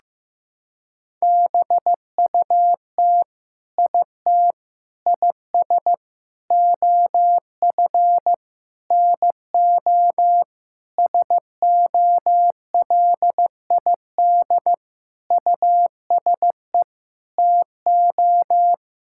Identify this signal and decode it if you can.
Morse code, heard as a keyed tone, about 15 wpm, 700 Hz. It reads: BUT IT IS OF NO SOLID USE TO
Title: Sentences from Top 1600 Words 15wpm